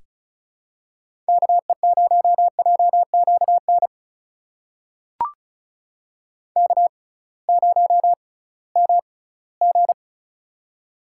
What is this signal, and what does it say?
Morse code, heard as a keyed tone, about 36 wpm, spaced as for 20 wpm, 700 Hz. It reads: KE0JQN K0MG
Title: USA Call Signs - 6x Character Spacing - 35wpm